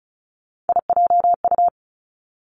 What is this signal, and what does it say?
Morse code, heard as a keyed tone, about 36 wpm, 700 Hz. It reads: IJU
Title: Three Challenging Letter-Number Combinations 35wpm